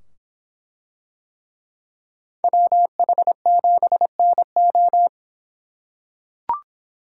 Morse code, heard as a keyed tone, about 24 wpm, 700 Hz. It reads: WH7NO